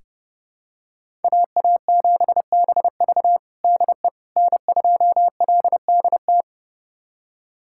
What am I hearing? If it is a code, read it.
Morse code, heard as a keyed tone, about 30 wpm, 700 Hz. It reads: AA7BV DE N2LDT
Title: USA Callsign DE Callsign - 5 Characters - 30wpm